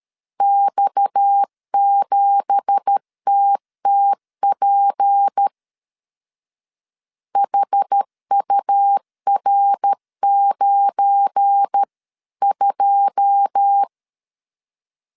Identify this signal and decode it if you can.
Morse code, heard as a keyed tone, about 13 wpm, 800 Hz. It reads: X7TTP HUR92
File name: CW / DET / 1 9 1500C S B 12wpm 3819